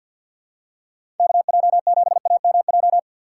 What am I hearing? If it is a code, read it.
Morse code, heard as a keyed tone, about 52 wpm, 700 Hz. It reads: KJ7AMJ
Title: USA Call Signs - 6 Characters - 4x Character Spacing - 50wpm